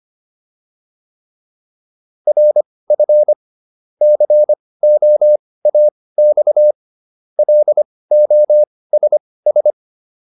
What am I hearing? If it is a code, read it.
Morse code, heard as a keyed tone, about 25 wpm, spaced as for 18 wpm, 600 Hz. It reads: RF COAX LOSS